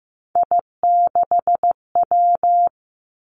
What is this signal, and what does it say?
Morse code, heard as a keyed tone, about 15 wpm, 700 Hz. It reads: I6W
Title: Three Challenging Letter-Number Combinations 15wpm